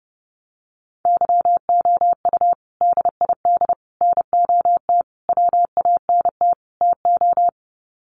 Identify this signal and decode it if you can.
Morse code, heard as a keyed tone, about 30 wpm, 700 Hz. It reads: YOU DID NOT WANT TO
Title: Sets of 5 Words Encore2 30wpm